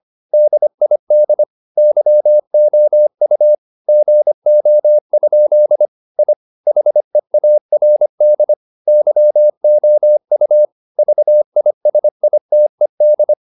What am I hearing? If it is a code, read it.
Morse code, heard as a keyed tone, about 25 wpm, 600 Hz. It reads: DID YOU GO? I HEARD YOU VISITED